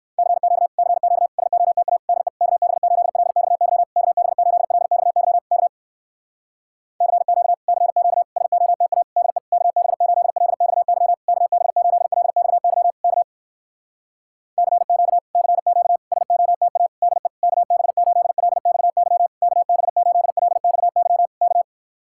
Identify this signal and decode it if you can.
Morse code, heard as a keyed tone, about 59 wpm, 700 Hz. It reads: CQ CQ SOTA DE KB8LXY KB8LXY K CQ CQ SOTA DE KB8LXY KB8LXY K CQ CQ SOTA DE KB8LXY KB8LXY K